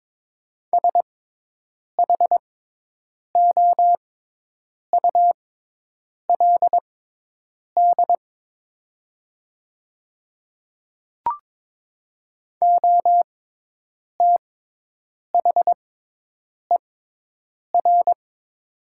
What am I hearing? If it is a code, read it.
Morse code, heard as a keyed tone, about 22 wpm, spaced as for 8 wpm, 700 Hz. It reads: SHOULD OTHER
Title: 6x Character Spacing - Top 300 Words - 22wpm